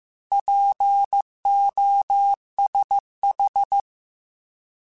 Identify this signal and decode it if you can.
Morse code, heard as a keyed tone, about 15 wpm, 800 Hz. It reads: POSH